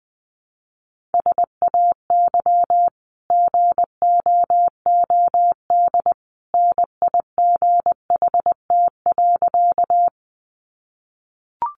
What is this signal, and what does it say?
Morse code, heard as a keyed tone, about 20 wpm, 700 Hz. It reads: SAY GOOD NIGHT.
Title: Sentences from Top 300 Words 20wpm